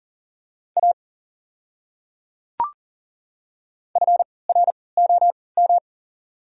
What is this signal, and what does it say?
Morse code, heard as a keyed tone, about 39 wpm, spaced as for 23 wpm, 690 Hz. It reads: A FROM
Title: 3x Character Spacing - Top 100 Words - 40wpm